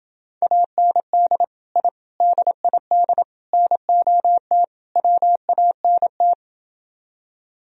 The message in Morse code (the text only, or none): AND I DID NOT WANT